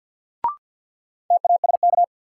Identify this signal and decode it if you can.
Morse code, heard as a keyed tone, about 49 wpm, 700 Hz. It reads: TASK
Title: Sentences from Top 1000 Words - Review Part 7 - New Word Drill - 50wpm